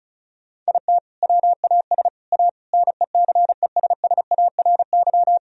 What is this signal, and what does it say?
Morse code, heard as a keyed tone, about 36 wpm, 700 Hz. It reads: IT WAS A NECESSARY
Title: Sentences from Top 1000 Words - Review Part 2 - 35wpm